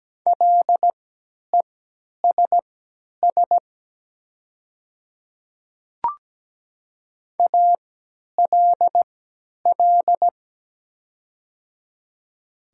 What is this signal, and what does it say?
Morse code, heard as a keyed tone, about 17 wpm, spaced as for 10 wpm, 700 Hz. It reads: LESS ALL
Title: 3x Character Spacing - Top 500 Words - 17wpm